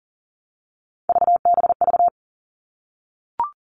Morse code, heard as a keyed tone, about 40 wpm, 700 Hz. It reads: VBV
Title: All Cut Numbers - 3-Digit 40wpm